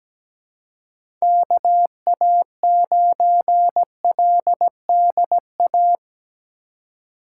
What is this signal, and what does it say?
Morse code, heard as a keyed tone, about 17 wpm, 700 Hz. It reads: KA9LDA
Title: USA Call Signs - 6 Characters - 2x Character Spacing - 17wpm